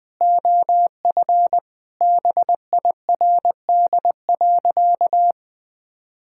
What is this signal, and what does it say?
Morse code, heard as a keyed tone, about 21 wpm, 700 Hz. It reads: OF BIRD.
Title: Sentences with 3-Word Phrases - 20wpm